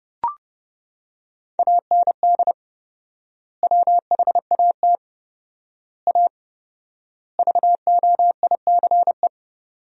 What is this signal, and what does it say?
Morse code, heard as a keyed tone, about 30 wpm, 700 Hz. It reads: AND WHAT A VOICE
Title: 4x Word-Spacing - Sentences from Top 500 Words - 30wpm